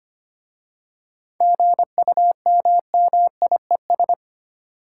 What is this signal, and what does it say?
Morse code, heard as a keyed tone, about 25 wpm, 700 Hz. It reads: GUMMIES
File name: practice / 25wpm / Easy Words with Hints - 25wpm